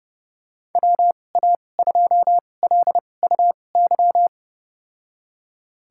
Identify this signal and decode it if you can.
Morse code, heard as a keyed tone, about 30 wpm, spaced as for 22 wpm, 700 Hz. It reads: WA2LUY
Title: USA Call Signs - 6 Characters - 2x Character Spacing - 30wpm